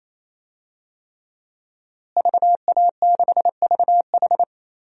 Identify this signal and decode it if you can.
Morse code, heard as a keyed tone, about 28 wpm, 700 Hz. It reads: VA6VH
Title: DXCC-100 Call Signs - 5 Characters - Repeated 3x - 28wpm